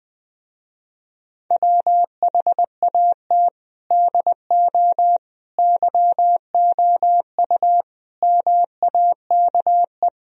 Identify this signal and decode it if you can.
Morse code, heard as a keyed tone, about 20 wpm, 700 Hz. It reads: WHAT DO YOU MAKE
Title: Speed-Racing - Sets of 5 Words 17wpm